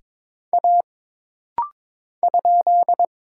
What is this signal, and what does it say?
Morse code, heard as a keyed tone, about 22 wpm, 700 Hz. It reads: A ?
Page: This rapid-fire timing is geared to push your ICR (Instant Character Recognition) skills.